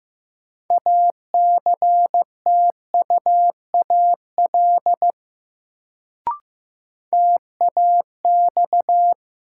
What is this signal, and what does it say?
Morse code, heard as a keyed tone, about 15 wpm, 700 Hz. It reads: ACTUAL TAX
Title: Sentences from Top 1000 Words - Review Part 5 - New Word Drill - 15wpm